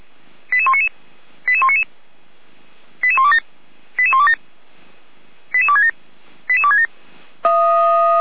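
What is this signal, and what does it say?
Five-tone selective call (ZVEI), heard as quick runs of five tones, tones of about 70 ms. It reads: 88199 88199 88127 88127 88267 88267
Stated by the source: SDR-received ZVEI five-tone selective call, address 88125.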